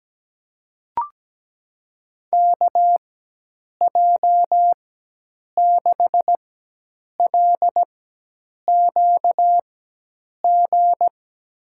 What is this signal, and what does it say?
Morse code, heard as a keyed tone, about 17 wpm, spaced as for 8 wpm, 700 Hz. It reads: KJ6LQG
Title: USA Call Signs - 6 Characters - 4x Character Spacing - 17wpm